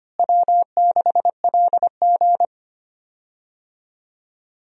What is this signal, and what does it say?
Morse code, heard as a keyed tone, about 25 wpm, 700 Hz. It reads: W6LG
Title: USA Call Signs - 4 Characters - Repeated 3x - 25wpm